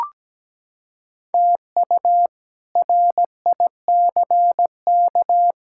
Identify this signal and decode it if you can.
Morse code, heard as a keyed tone, about 17 wpm, 700 Hz. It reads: TU RICK
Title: K1USN SST QSO Practice - 17wpm